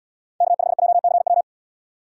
Morse code, heard as a keyed tone, about 69 wpm, 700 Hz. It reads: K5JCW